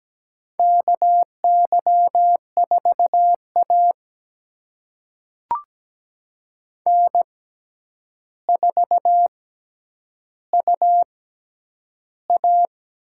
Morse code, heard as a keyed tone, about 17 wpm, spaced as for 9 wpm, 700 Hz. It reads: KY4A N4UA
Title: USA Call Signs - 4 Characters - 6x Character Spacing - 17wpm